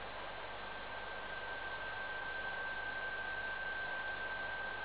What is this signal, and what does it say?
PSK31, about 31.25 baud.